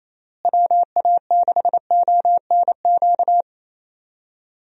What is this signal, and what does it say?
Morse code, heard as a keyed tone, about 27 wpm, 700 Hz. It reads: WA6ONQ
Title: USA Call Signs - 6 Characters - Repeated 2x - 28wpm